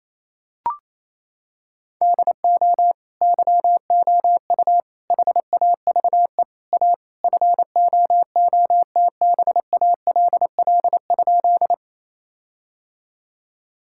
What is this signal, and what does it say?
Morse code, heard as a keyed tone, about 28 wpm, 700 Hz. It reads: DO YOU HAVE A FOOTBALL?
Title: Sentences from Top 1600 Words 28wpm